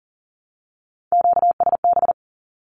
Morse code, heard as a keyed tone, about 40 wpm, 700 Hz. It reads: QSB
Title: Q-Codes 40wpm